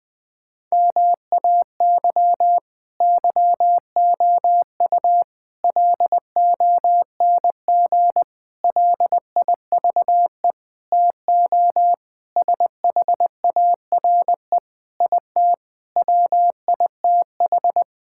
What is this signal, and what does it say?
Morse code, heard as a keyed tone, about 21 wpm, 700 Hz. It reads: MAY YOU LONG LIVE TO SHARE IT WITH
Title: Sentences from Top 1000 Words - Review Part 6 - 20wpm